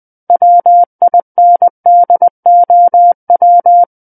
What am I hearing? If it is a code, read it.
Morse code, heard as a keyed tone, about 20 wpm, 700 Hz. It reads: WINDOW
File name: 20 WPM